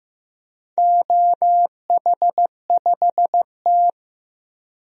Morse code, heard as a keyed tone, about 15 wpm, 700 Hz. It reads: OH5T
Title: DXCC-100 Call Signs - 4 Characters - 15wpm